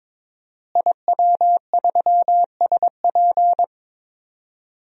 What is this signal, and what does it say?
Morse code, heard as a keyed tone, about 22 wpm, 700 Hz. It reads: IW3SP
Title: DXCC-100 Call Signs - 5 Characters - Repeated 2x - 22wpm